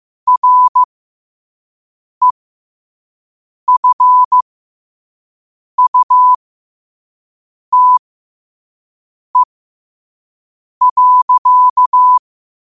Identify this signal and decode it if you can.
Morse code, heard as a keyed tone, about 15 wpm, spaced as for 5 wpm, 1000 Hz. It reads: REFUTE.
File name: qotdfiles / 5wpm